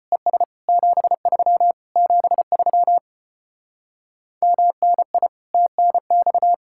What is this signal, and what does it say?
Morse code, heard as a keyed tone, about 33 wpm, 700 Hz. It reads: ES 73 73 MNI TNX